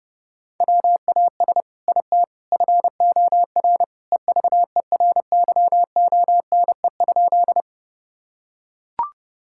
Morse code, heard as a keyed tone, about 30 wpm, 700 Hz. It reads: WAS IT FOR EVERYONE?
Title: Sentences from Top 1000 Words - Review Part 5 - 30wpm